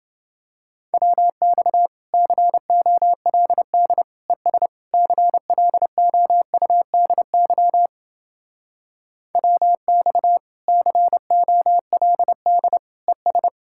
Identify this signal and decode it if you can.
Morse code, heard as a keyed tone, about 28 wpm, 700 Hz. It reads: WX COLD ES CLOUDY WX COLD ES